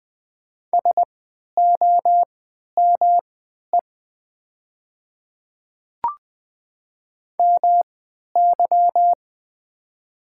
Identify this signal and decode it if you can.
Morse code, heard as a keyed tone, about 20 wpm, spaced as for 11 wpm, 700 Hz. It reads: SOME MY
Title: 3x Character Spacing - Top 200 Words - 20wpm